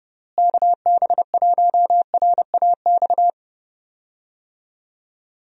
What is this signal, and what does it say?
Morse code, heard as a keyed tone, about 30 wpm, 700 Hz. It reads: KB1RAX
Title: USA Call Signs - 6 Characters - Repeated 2x - 30wpm